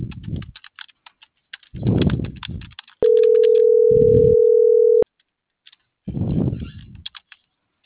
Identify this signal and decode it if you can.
Ringback tone (North America), 440 + 480 Hz.